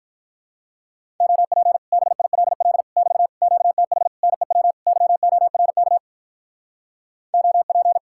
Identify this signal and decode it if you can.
Morse code, heard as a keyed tone, about 46 wpm, 700 Hz. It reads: OP BILL <BT> QTH NEW YORK OP